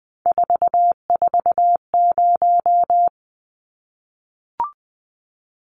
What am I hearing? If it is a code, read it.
Morse code, heard as a keyed tone, about 20 wpm, 700 Hz. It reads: 440
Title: Numbers - 3-Digit 20wpm